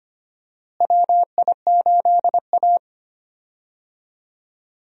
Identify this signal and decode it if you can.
Morse code, heard as a keyed tone, about 25 wpm, 700 Hz. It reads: WI8A
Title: USA Call Signs - 4 Characters - Repeated 3x - 25wpm